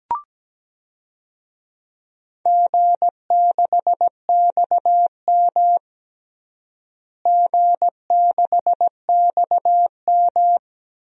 Morse code, heard as a keyed tone, about 17 wpm, 700 Hz. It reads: G6XM G6XM
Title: DXCC-100 Call Signs - 4 Characters - Repeated 3x - 17wpm